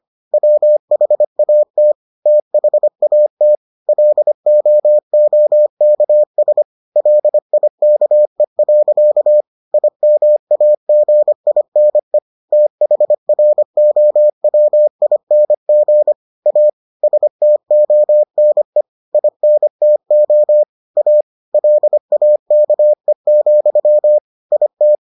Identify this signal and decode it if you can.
Morse code, heard as a keyed tone, about 25 wpm, 600 Hz. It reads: WHAT THAT LOOKS LIKE. IMAGINE THROWING A STONE INTO A LAKE, IT